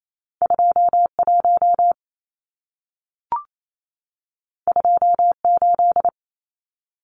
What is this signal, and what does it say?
Morse code, heard as a keyed tone, about 28 wpm, 700 Hz. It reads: 21 28
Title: Numbers - 2-Digit 28wpm